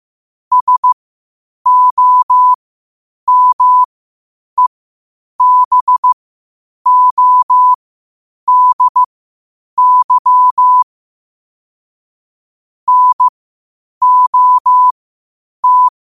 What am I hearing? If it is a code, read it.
Morse code, heard as a keyed tone, about 15 wpm, spaced as for 8 wpm, 1000 Hz. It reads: SOMEBODY NOT
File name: qotdfiles / 8wpm